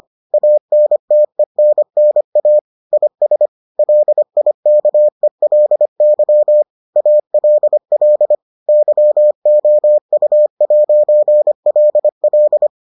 Morse code, heard as a keyed tone, about 25 wpm, 600 Hz. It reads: ANTENNA IS LIKELY ALL YOU'LL